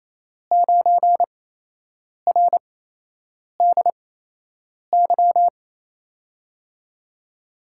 Morse code, heard as a keyed tone, about 28 wpm, spaced as for 8 wpm, 700 Hz. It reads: 9RDY
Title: USA Call Signs - 5 Characters - 8x Character Spacing - 28wpm